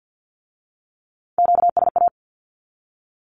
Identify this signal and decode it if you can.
Morse code, heard as a keyed tone, about 50 wpm, 700 Hz. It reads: QSA
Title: Q-Codes with Spoken Meaning - 50wpm